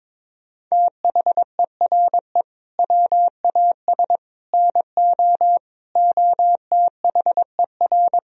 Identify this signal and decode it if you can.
Morse code, heard as a keyed tone, about 21 wpm, 700 Hz. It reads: THERE WAS NO OTHER
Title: Sentences from Top 1000 Words - Review Part 3 - 22wpm